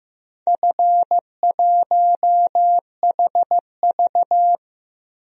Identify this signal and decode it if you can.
Morse code, heard as a keyed tone, about 15 wpm, 700 Hz. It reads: F1HV
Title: DXCC-100 Call Signs - 4 Characters - Repeated 3x - 15wpm